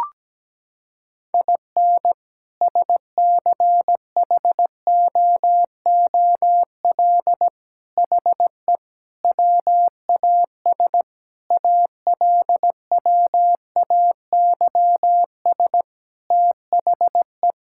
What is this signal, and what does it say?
Morse code, heard as a keyed tone, about 17 wpm, 700 Hz. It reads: IN SCHOOL HE WAS ALWAYS THE
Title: Sentences from Top 1000 Words - Review Part 5 - 17wpm